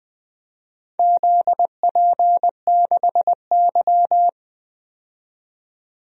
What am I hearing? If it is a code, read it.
Morse code, heard as a keyed tone, about 20 wpm, 700 Hz. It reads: ZP6Y
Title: DXCC-100 Call Signs - 4 Characters - 20wpm